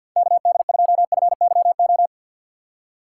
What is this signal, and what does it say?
Morse code, heard as a keyed tone, about 51 wpm, 700 Hz. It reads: KD2FYO
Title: USA Call Signs - 6 Characters - 4x Character Spacing - 50wpm